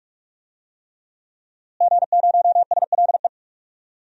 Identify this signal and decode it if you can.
Morse code, heard as a keyed tone, about 46 wpm, 700 Hz. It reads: G0SLE
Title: DXCC-100 Call Signs - 5 Characters - Repeated 3x - 45wpm